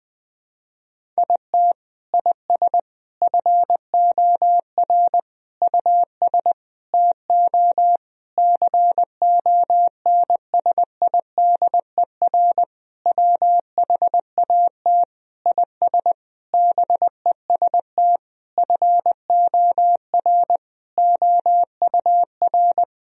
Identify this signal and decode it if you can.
Morse code, heard as a keyed tone, about 20 wpm, 700 Hz. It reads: IT IS FOR US TO CONSIDER WHAT IS BEST FOR OUR
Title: Sentences from Top 1000 Words - Review Part 4 - 20wpm